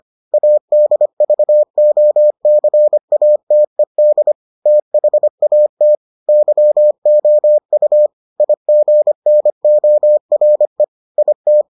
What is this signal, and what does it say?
Morse code, heard as a keyed tone, about 25 wpm, 600 Hz. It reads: ADVOCATED THAT YOU IGNORE IT